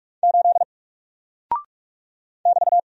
Morse code, heard as a keyed tone, about 46 wpm, 700 Hz. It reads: 8 <BT>
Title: Single Character (Letter:Number:Punctuation:Prosign) Alt 45wpm